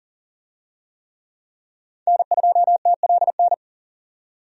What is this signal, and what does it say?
Morse code, heard as a keyed tone, about 40 wpm, 700 Hz. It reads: N2TLN
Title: USA Call Signs - 5 Characters - 40wpm